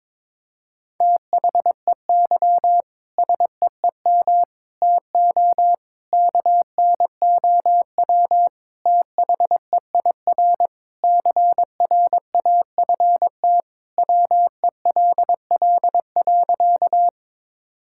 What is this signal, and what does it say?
Morse code, heard as a keyed tone, about 22 wpm, 700 Hz. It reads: THEY SEEM TO KNOW THEIR CRAFT WELL.
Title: Sentences from Top 1000 Words - Review Part 6 - 22wpm